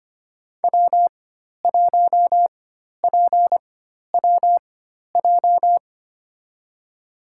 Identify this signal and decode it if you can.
Morse code, heard as a keyed tone, about 25 wpm, spaced as for 12 wpm, 700 Hz. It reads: W1PWJ
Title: USA Call Signs - 5 Characters - 4x Character Spacing - 25wpm